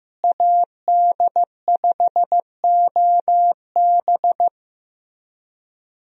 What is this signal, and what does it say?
Morse code, heard as a keyed tone, about 15 wpm, 700 Hz. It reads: AD5OB
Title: USA Call Signs - 5 Characters - Repeated 2x - 15wpm